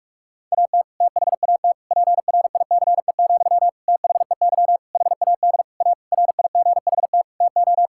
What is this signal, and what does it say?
Morse code, heard as a keyed tone, about 46 wpm, 700 Hz. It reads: AT THAT PRICE, THEY HAD A RIGHT TO
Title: Sentences from Top 1000 Words - Review Part 5 - 45wpm